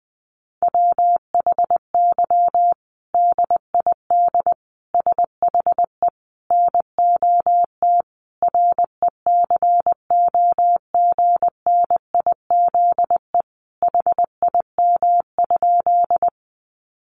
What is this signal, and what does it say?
Morse code, heard as a keyed tone, about 20 wpm, 700 Hz. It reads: WHY DID SHE NOT RECOGNIZE HIM?
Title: Sentences from Top 1400 Words 20wpm